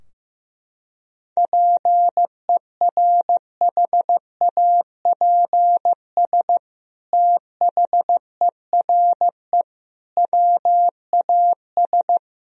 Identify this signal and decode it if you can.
Morse code, heard as a keyed tone, about 15 wpm, 700 Hz. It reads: PERHAPS THERE WAS
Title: Sentences from Top 1000 Words - Review Part 4 - 15wpm